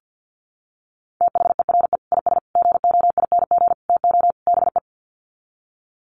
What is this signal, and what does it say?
Morse code, heard as a keyed tone, about 50 wpm, 700 Hz. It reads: THERE IS GOING TO BE
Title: Sets of 5 Words Encore 50wpm